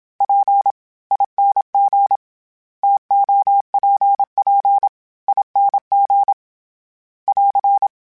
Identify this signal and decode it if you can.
Morse code, heard as a keyed tone, about 26 wpm, spaced as for 22 wpm, 800 Hz. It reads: PING TOPPING <AR>
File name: Variable Spacing / Variable Spacing-mp3s / TOs @25